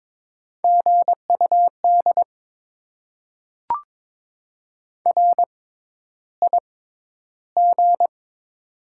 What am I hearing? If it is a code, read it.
Morse code, heard as a keyed tone, about 22 wpm, spaced as for 11 wpm, 700 Hz. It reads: GUD RIG
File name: Top 20 QSO Elements - 6x Character Spacing - 22wpm